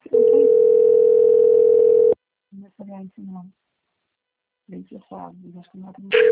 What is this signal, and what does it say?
Ringback tone (North America), 440 + 480 Hz, with speech.
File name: out- - - - - .20053